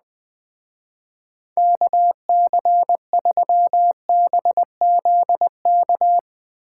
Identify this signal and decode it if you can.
Morse code, heard as a keyed tone, about 21 wpm, 700 Hz. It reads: KC3BZK